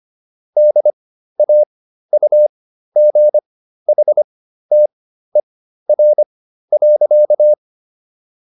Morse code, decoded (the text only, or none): DAUGHTER.